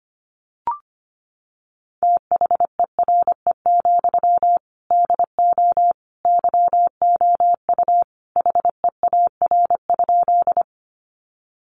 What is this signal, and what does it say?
Morse code, heard as a keyed tone, about 25 wpm, 700 Hz. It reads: THERE, DO YOU HEAR?
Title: Sentences from Top 300 Words 25wpm